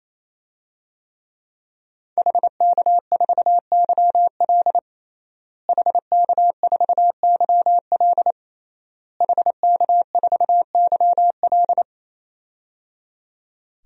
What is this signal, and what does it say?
Morse code, heard as a keyed tone, about 28 wpm, 700 Hz. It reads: HK4YL HK4YL HK4YL
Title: DXCC-100 Call Signs - 5 Characters - Repeated 3x - 28wpm